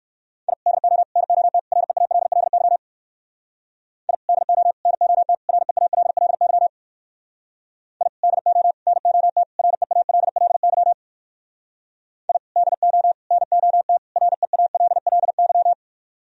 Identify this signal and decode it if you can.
Morse code, heard as a keyed tone, about 65 wpm, 700 Hz. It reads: I DO NOT REALLY I DO NOT REALLY I DO NOT REALLY I DO NOT REALLY